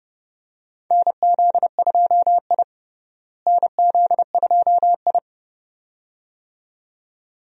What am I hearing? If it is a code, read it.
Morse code, heard as a keyed tone, about 32 wpm, 700 Hz. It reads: NZ2I NZ2I
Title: USA Call Signs - 4 Characters - Repeated 2x - 30wpm